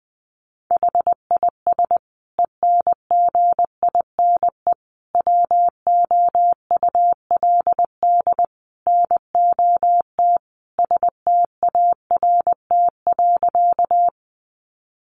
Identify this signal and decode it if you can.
Morse code, heard as a keyed tone, about 20 wpm, 700 Hz. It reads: HIS ENGINE WOULD NOT START.
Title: Sentences from Top 1500 Words 20wpm